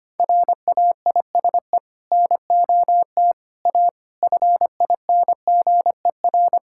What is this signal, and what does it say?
Morse code, heard as a keyed tone, about 25 wpm, 700 Hz. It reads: RAISE NOT A FINGER
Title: Sentences from Top 1600 Words 25wpm